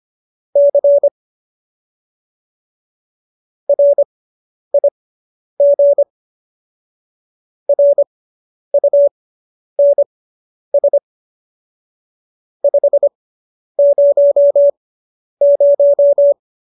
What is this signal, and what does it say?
Morse code, heard as a keyed tone, about 25 wpm, spaced as for 10 wpm, 570 Hz. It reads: C RIG RUNS 500